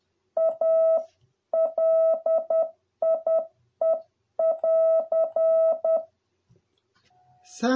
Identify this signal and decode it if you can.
Morse code, heard as a keyed tone, about 10 wpm, 650 Hz. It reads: A LIE <AR>